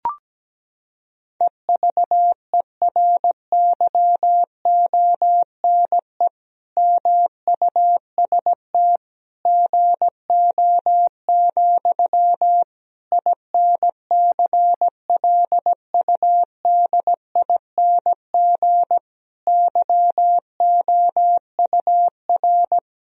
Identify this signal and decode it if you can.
Morse code, heard as a keyed tone, about 17 wpm, 700 Hz. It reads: EVERYONE MUST GO, INCLUDING YOUR
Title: Sentences from Top 1000 Words - Review Part 4 - 17wpm